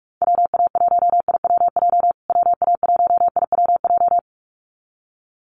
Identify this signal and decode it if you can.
Morse code, heard as a keyed tone, about 44 wpm, 700 Hz. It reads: WA1IWJ WA1IWJ
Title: USA Call Signs Repeated 2x 45wpm